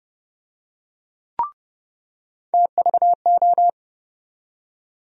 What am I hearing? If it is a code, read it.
Morse code, heard as a keyed tone, about 31 wpm, 700 Hz. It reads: TVO
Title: USA Call Sign Postfixes - 3 Letters 30wpm